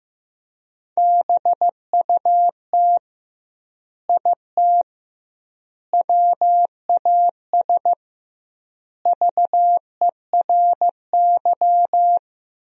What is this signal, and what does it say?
Morse code, heard as a keyed tone, about 15 wpm, 700 Hz. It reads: BUT IT WAS VERY
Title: 2x Word-Spacing - Sentences from Top 300 Words - 15wpm